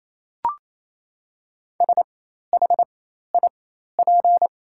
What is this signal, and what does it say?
Morse code, heard as a keyed tone, about 28 wpm, spaced as for 13 wpm, 700 Hz. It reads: SHIP
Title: 4x Character Spacing - Top 500 Words - 28wpm